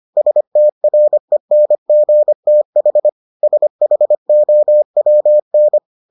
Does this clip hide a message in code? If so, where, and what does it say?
Yes: Morse code from 0.2 to 5.8 s, heard as a keyed tone, about 25 wpm, 600 Hz. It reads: STRENGTH SHOWN